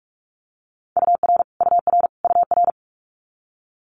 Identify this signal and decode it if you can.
Morse code, heard as a keyed tone, about 46 wpm, 700 Hz. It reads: UR UR UR
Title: Multi-letter Phonemes Repeated 3x 45wpm